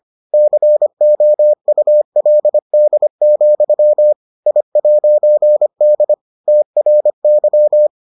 Morse code, heard as a keyed tone, about 25 wpm, 600 Hz. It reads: COULD, I'D TRY